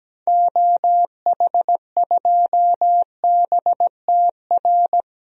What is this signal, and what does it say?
Morse code, heard as a keyed tone, about 17 wpm, 700 Hz. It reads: OH2BTR
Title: DXCC-100 Call Signs - 6 Characters - 17wpm